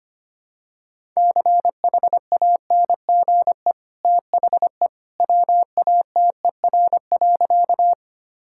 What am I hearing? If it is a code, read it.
Morse code, heard as a keyed tone, about 25 wpm, 700 Hz. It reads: CHANGE THE WATER.
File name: Easy Sentences with Hints - 25wpm